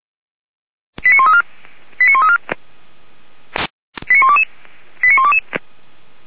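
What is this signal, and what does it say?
Five-tone selective call (ZVEI), heard as quick runs of five tones, tones of about 70 ms. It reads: 89134 89134 89122 89122
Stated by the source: Five-tone ZVEI call received by SDR, recorded by Feuerwehr Orlamünde (fire brigade).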